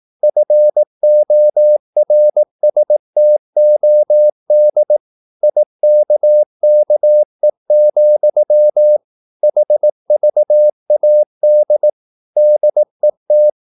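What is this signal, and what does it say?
Morse code, heard as a keyed tone, about 18 wpm, 600 Hz. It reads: FORSTOD IKKE, HVAD DET
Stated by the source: Slow (QRS) Morse practice at 18 wpm, CW med Gnister.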